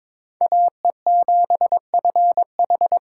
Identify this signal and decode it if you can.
Morse code, heard as a keyed tone, about 26 wpm, 700 Hz. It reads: AE7FH
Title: Speed Racing - USA 2x2 Call Signs 22wpm